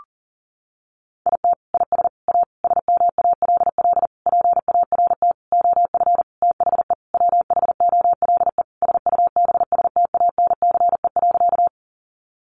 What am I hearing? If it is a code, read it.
Morse code, heard as a keyed tone, about 40 wpm, 700 Hz. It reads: IT IS A SMALL PART OF THE WHOLE SUBSTANCE.
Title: Sentences from Top 1500 Words 40wpm